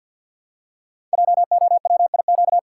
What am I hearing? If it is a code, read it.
Morse code, heard as a keyed tone, about 52 wpm, 700 Hz. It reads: JOWIQ
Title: Code Groups - 5 Letters - 50wpm